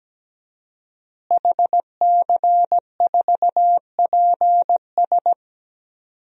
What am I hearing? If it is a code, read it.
Morse code, heard as a keyed tone, about 17 wpm, 700 Hz. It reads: HC4PS
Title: DXCC-100 Call Signs - 5 Characters - Repeated 3x - 17wpm